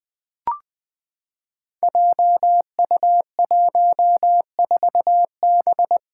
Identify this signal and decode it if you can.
Morse code, heard as a keyed tone, about 20 wpm, 700 Hz. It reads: JU14B
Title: Five Challenging Letter-Number Combinations 20wpm